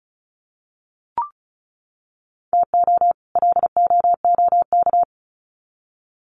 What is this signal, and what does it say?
Morse code, heard as a keyed tone, about 34 wpm, 700 Hz. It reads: TO LOOK
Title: Sets of 2 Words 35wpm